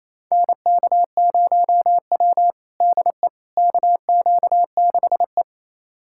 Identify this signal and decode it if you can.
Morse code, heard as a keyed tone, about 28 wpm, 700 Hz. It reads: NK0W DE KQ6E
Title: USA Callsign DE Callsign - 4 Characters - 28wpm